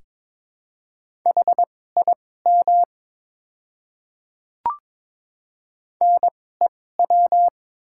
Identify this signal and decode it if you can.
Morse code, heard as a keyed tone, about 22 wpm, spaced as for 16 wpm, 700 Hz. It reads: HIM NEW
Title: 2x Character Spacing - Top 200 Words - 22wpm